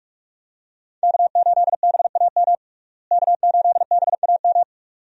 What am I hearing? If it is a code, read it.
Morse code, heard as a keyed tone, about 46 wpm, 700 Hz. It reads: K8BAM K8BAM
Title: USA Call Signs - 5 Characters - Repeated 2x - 45wpm